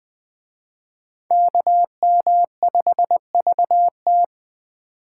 Morse code, heard as a keyed tone, about 21 wpm, 700 Hz. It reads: KM5VT
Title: Speed Racing - USA 2x2 Call Signs 17wpm